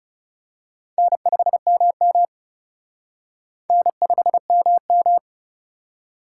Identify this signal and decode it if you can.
Morse code, heard as a keyed tone, about 34 wpm, 700 Hz. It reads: N5MM N5MM
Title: Speed Racing - USA 1x2 Call Signs 30wpm